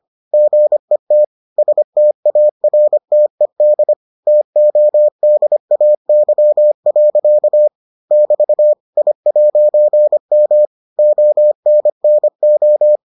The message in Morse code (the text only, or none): GET STARTED TODAY. <BT> I'M ONNO